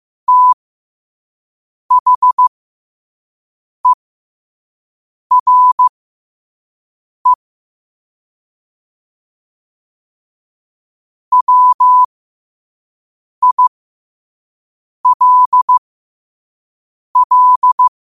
Morse code, heard as a keyed tone, about 15 wpm, spaced as for 5 wpm, 1000 Hz. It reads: THERE WILL
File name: qotdfiles / 5wpm